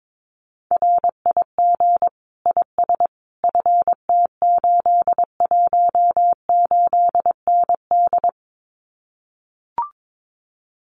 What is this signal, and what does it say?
Morse code, heard as a keyed tone, about 22 wpm, 700 Hz. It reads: RIG IS FT818ND
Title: Common QSO Phrases Encore 22wpm